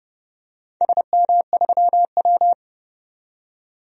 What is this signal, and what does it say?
Morse code, heard as a keyed tone, about 30 wpm, 700 Hz. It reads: SM3W
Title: DXCC-100 Call Signs - 4 Characters - 30wpm